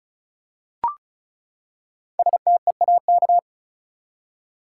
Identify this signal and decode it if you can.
Morse code, heard as a keyed tone, about 36 wpm, 700 Hz. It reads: STEAK
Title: Five Letter Words - 35wpm